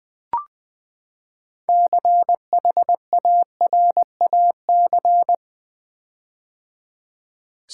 Morse code, heard as a keyed tone, about 20 wpm, 700 Hz. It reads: CHARAC
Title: Build A Word - 6 Letters At A Time - 12-Letter Words - 20wpm